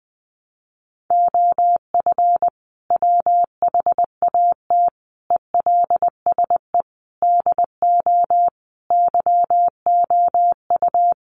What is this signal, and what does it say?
Morse code, heard as a keyed tone, about 20 wpm, 700 Hz. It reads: OF WHAT ELSE DO YOU
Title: Sentences from Top 1400 Words 20wpm